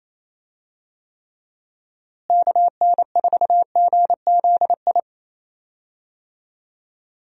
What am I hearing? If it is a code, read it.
Morse code, heard as a keyed tone, about 28 wpm, 700 Hz. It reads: KN4GZI
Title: USA Call Signs - 6 Characters - Repeated 3x - 28wpm